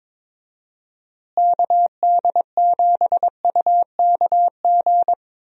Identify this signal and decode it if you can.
Morse code, heard as a keyed tone, about 22 wpm, 700 Hz. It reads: KD7UKG
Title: USA Call Signs - 6 Characters - Repeated 2x - 22wpm